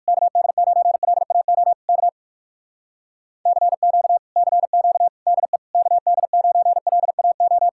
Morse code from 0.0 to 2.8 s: KD9LAO K

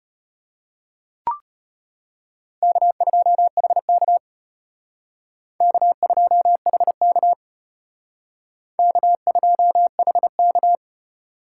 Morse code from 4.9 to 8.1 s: K2HK

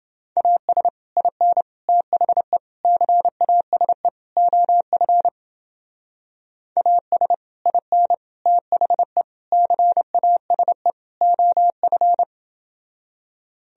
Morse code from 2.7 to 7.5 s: CASE OF AS

